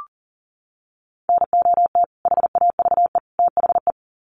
Morse code, heard as a keyed tone, about 40 wpm, 700 Hz. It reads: NOT HAVE THE
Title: Sets of 3 Words 40wpm